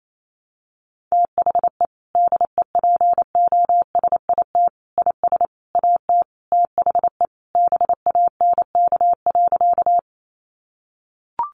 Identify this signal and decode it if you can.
Morse code, heard as a keyed tone, about 28 wpm, 700 Hz. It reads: THE DEPOSIT IS AT THE BANK.
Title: Sentences from Top 1300 Words 28wpm